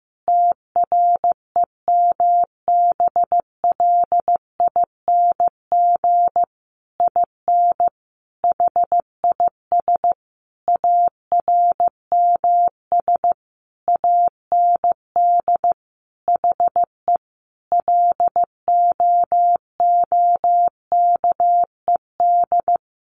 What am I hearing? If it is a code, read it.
Morse code, heard as a keyed tone, about 15 wpm, 700 Hz. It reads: TREMBLING IN HIS ARMS AND HE LOOKED
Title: A Sound Of Thunder 15wpm